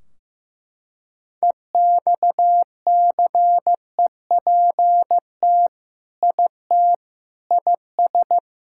Morse code, heard as a keyed tone, about 15 wpm, 700 Hz. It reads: EXCEPT IT IS
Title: Sentences from Top 1600 Words 15wpm